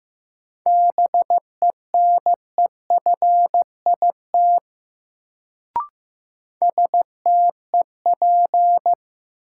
Morse code, heard as a keyed tone, about 16 wpm, 700 Hz. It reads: BENEFIT STEP
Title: Sets of 5 Words Encore - New Word Drill - 15wpm